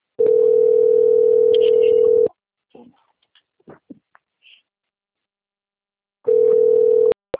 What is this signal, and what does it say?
Ringback tone (North America), 440 + 480 Hz, with speech.
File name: out- - - - - .43949